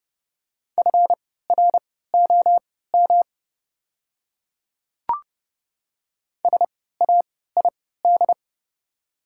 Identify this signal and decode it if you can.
Morse code, heard as a keyed tone, about 31 wpm, spaced as for 17 wpm, 690 Hz. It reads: FROM SAID
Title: 3x Character Spacing - Top 300 Words - 30wpm